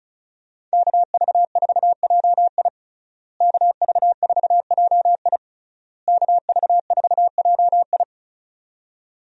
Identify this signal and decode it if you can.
Morse code, heard as a keyed tone, about 36 wpm, 700 Hz. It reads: KV4JI KV4JI KV4JI
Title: USA Call Signs - 5 Characters - Repeated 3x - 35wpm